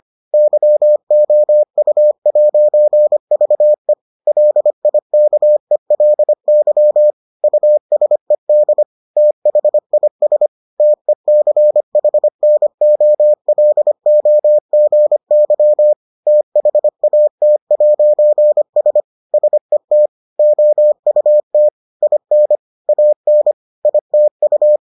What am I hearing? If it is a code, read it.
Morse code, heard as a keyed tone, about 25 wpm, 600 Hz. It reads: YOU'VE LIKELY USED THIS TECHNOLOGY THAT'S SET OUT IN AN ITU